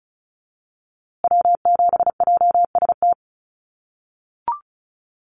Morse code, heard as a keyed tone, about 34 wpm, 700 Hz. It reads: W7JST
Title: USA Call Signs Repeated 2x 35wpm